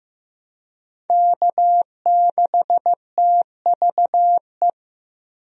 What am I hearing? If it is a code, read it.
Morse code, heard as a keyed tone, about 15 wpm, 700 Hz. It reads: K6TVE
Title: USA Call Signs - 5 Characters - 15wpm